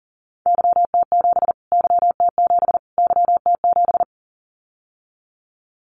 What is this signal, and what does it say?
Morse code, heard as a keyed tone, about 40 wpm, 700 Hz. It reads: YT7 YT7 YT7
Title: Callsign Prefixes for Top 25 DXCC - Repeated 3x - 40wpm